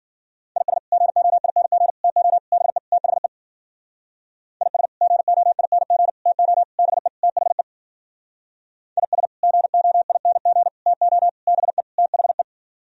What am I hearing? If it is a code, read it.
Morse code, heard as a keyed tone, about 52 wpm, 700 Hz. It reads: IS GOING TO BE THE IS GOING TO BE THE IS GOING TO BE THE